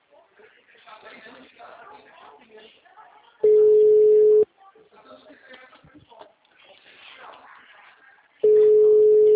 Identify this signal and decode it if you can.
Ringback tone (Europe, 425 Hz), with speech.
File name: out- -2013- - - .65